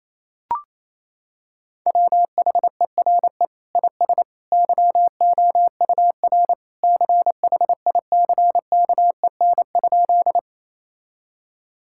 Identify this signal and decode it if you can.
Morse code, heard as a keyed tone, about 28 wpm, 700 Hz. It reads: WHERE IS YOUR CHICKEN?
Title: Sentences from Top 1000 Words - Review Part 5 - 28wpm